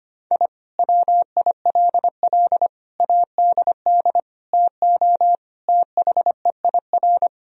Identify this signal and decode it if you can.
Morse code, heard as a keyed tone, about 25 wpm, 700 Hz. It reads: I WILL ADD TO THEIR
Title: Sentences from Top 1000 Words - Review Part 1 - 25wpm